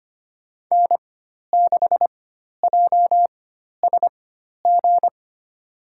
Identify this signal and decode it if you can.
Morse code, heard as a keyed tone, about 25 wpm, spaced as for 12 wpm, 700 Hz. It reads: N6JSG
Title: USA Call Signs - 4x Character Spacing - 25wpm